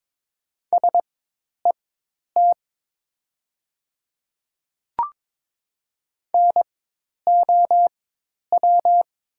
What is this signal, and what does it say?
Morse code, heard as a keyed tone, about 22 wpm, spaced as for 10 wpm, 700 Hz. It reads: SET NOW
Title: 4x Character Spacing - Top 200 Words - 22wpm